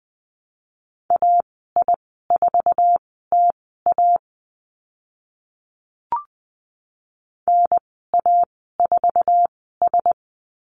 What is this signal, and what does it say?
Morse code, heard as a keyed tone, about 20 wpm, spaced as for 14 wpm, 700 Hz. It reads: AI4TA NA4S
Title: 048 Lesson 9g - Number 4 - Call Signs 20wpm